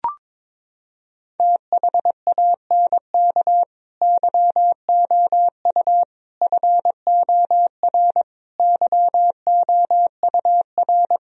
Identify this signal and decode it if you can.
Morse code, heard as a keyed tone, about 22 wpm, 700 Hz. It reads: THANK YOU FOR YOUR
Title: Sentences from Top 1000 Words - Review Part 3 - 22wpm